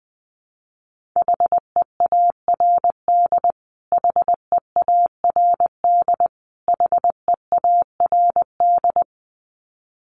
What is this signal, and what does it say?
Morse code, heard as a keyed tone, about 19 wpm, 700 Hz. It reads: HEARD HEARD HEARD
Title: Top 500 Words Repeated 20wpm